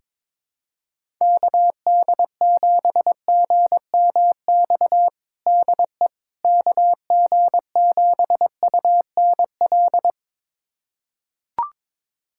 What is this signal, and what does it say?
Morse code, heard as a keyed tone, about 22 wpm, 700 Hz. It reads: KD7GMX DE KG7UNL
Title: USA Callsign DE Callsign - 6 Characters - 22wpm